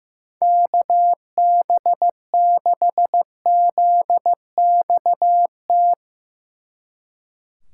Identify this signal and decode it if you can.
Morse code, heard as a keyed tone, about 16 wpm, 700 Hz. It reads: KB6ZXT
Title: Speed Racing - USA 2x3 Call Signs 15wpm